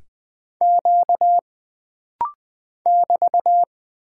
Morse code, heard as a keyed tone, about 20 wpm, 700 Hz. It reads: Q <BT>
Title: Single Character (Letter:Number:Punctuation:Prosign) - Rapid-Fire - 20wpm